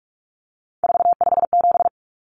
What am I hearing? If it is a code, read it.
Morse code, heard as a keyed tone, about 44 wpm, 700 Hz. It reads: 457
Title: Numbers - 3-Digit 45wpm